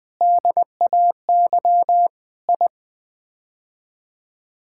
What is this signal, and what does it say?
Morse code, heard as a keyed tone, about 20 wpm, 700 Hz. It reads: DAY I
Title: Sets of 8 Words - 20wpm